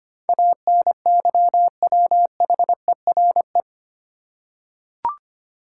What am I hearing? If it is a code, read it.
Morse code, heard as a keyed tone, about 25 wpm, 700 Hz. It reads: ANYWHERE
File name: Eight Letter Words - 25wpm